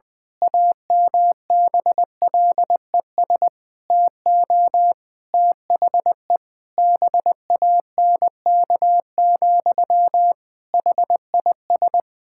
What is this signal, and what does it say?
Morse code, heard as a keyed tone, about 20 wpm, 700 Hz. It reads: AMBLES TO THE BANK, HIS
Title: Pangram Challenge v2 - 20wpm